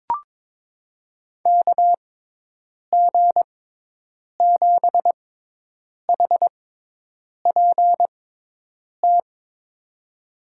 Morse code, heard as a keyed tone, about 22 wpm, spaced as for 8 wpm, 700 Hz. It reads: KG7HPT
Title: USA Call Signs - 6x Character Spacing - 22wpm